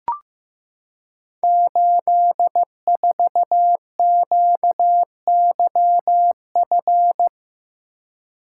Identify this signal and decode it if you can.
Morse code, heard as a keyed tone, about 15 wpm, 700 Hz. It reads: 84QYF
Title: Code Groups - 5 Letter:Numbers - 15wpm